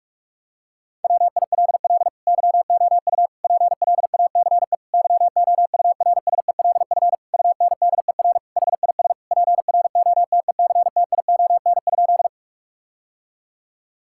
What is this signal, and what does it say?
Morse code, heard as a keyed tone, about 41 wpm, 700 Hz. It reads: WILL YOU PLACE YOURSELF UNDER HIS PROTECTION?
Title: Sentences from Top 1600 Words 45wpm